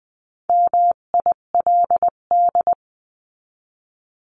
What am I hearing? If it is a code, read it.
Morse code, heard as a keyed tone, about 20 wpm, 700 Hz. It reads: MILD
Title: 258 Lesson 42c - Words 16-20wpm